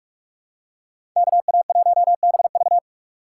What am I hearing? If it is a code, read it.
Morse code, heard as a keyed tone, about 46 wpm, 700 Hz. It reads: KA1BV
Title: USA Call Signs - 5 Characters - 6x Character Spacing - 45wpm